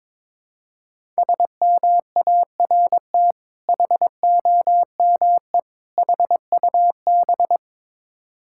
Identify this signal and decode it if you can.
Morse code, heard as a keyed tone, about 23 wpm, 690 Hz. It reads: SMART HOME HUB
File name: Difficult Words with Hints - 22wpm